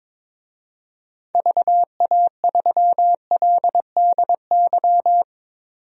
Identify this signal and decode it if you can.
Morse code, heard as a keyed tone, about 22 wpm, 700 Hz. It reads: VA3LDY
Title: DXCC-100 Call Signs - 6 Characters - 22wpm